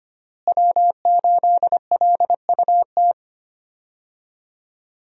Morse code, heard as a keyed tone, about 25 wpm, 700 Hz. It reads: W8LUT
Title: USA Call Signs - 5 Characters - Repeated 3x - 25wpm